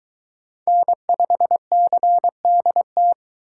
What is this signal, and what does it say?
Morse code, heard as a keyed tone, about 22 wpm, 700 Hz. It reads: N5CDT